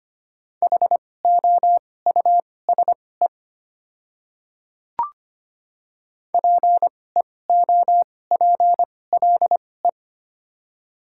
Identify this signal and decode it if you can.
Morse code, heard as a keyed tone, about 25 wpm, spaced as for 18 wpm, 700 Hz. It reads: HOUSE PEOPLE
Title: 2x Character Spacing - Top 300 Words - 25wpm